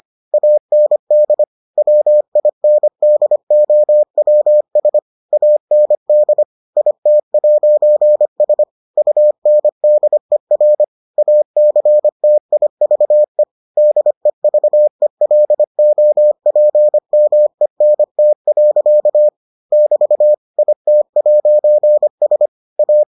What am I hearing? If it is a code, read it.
Morse code, heard as a keyed tone, about 25 wpm, 600 Hz. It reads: AND WINDOWS AND IT'S UNDER ACTIVE DEVELOPMENT. <BT> IT'S A